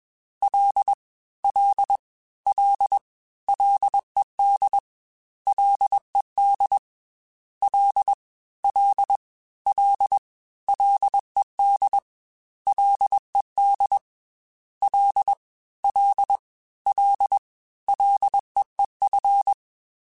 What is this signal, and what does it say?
Morse code, heard as a keyed tone, about 21 wpm, 800 Hz. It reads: L L L LED LED L L L LED LED L L L LEEF
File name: Ls / Single L m3s / Single L Prefixe 3x2 Twice @20